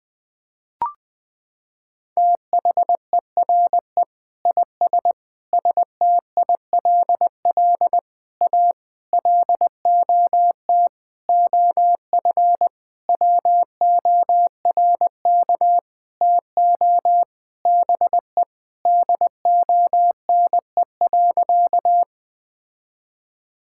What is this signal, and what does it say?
Morse code, heard as a keyed tone, about 20 wpm, 700 Hz. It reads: THERE IS STILL A LOT OF WORK TO BE DONE.
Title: Sentences with 9-Word Phrases - 20wpm